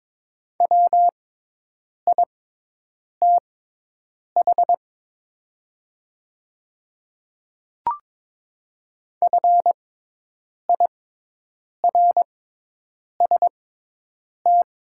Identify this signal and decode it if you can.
Morse code, heard as a keyed tone, about 22 wpm, spaced as for 8 wpm, 700 Hz. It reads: WITH FIRST
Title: 6x Character Spacing - Top 100 Words - 22wpm